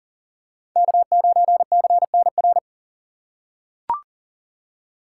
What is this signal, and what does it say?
Morse code, heard as a keyed tone, about 42 wpm, 700 Hz. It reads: K9CNR
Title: USA Call Signs - 5 Characters - 6x Character Spacing - 40wpm